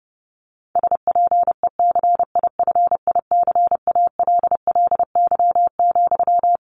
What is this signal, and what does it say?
Morse code, heard as a keyed tone, about 30 wpm, 700 Hz. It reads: SPECIFICALLY,
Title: Sentences from Top 1100 Words 30wpm